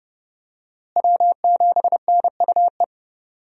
Morse code, heard as a keyed tone, about 30 wpm, 700 Hz. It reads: W7NUE
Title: USA Call Signs - 5 Characters - Repeated 2x - 30wpm